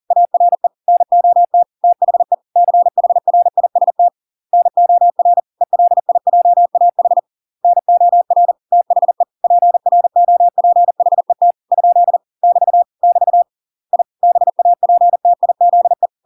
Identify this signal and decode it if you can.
Morse code, heard as a keyed tone, about 40 wpm, 700 Hz. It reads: ARE NOT THE CHRIST NOR ELIJAH NOR THE PROPHET ? <BT> <BT> I BAPTIZE